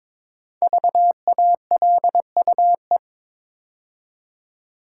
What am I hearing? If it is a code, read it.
Morse code, heard as a keyed tone, about 22 wpm, 700 Hz. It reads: VALUE
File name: 5 Letter Words - First 3 Letters Given - 22wpm